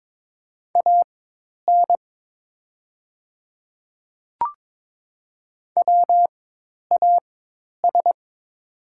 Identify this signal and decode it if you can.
Morse code, heard as a keyed tone, about 22 wpm, spaced as for 10 wpm, 700 Hz. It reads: AN WAS
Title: 4x Character Spacing - Top 100 Words - 22wpm